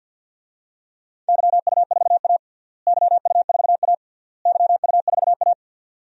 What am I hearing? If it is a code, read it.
Morse code, heard as a keyed tone, about 50 wpm, 700 Hz. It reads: YU4A YU4A YU4A
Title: DXCC-100 Call Signs - 4 Characters - Repeated 3x - 50wpm